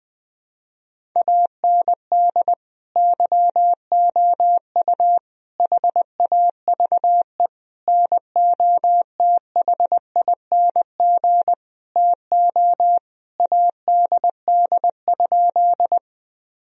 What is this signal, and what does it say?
Morse code, heard as a keyed tone, about 20 wpm, 700 Hz. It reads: AND YOU HAVE NOTHING TO ADD?
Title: Sentences from Top 1000 Words - Review Part 1 - 20wpm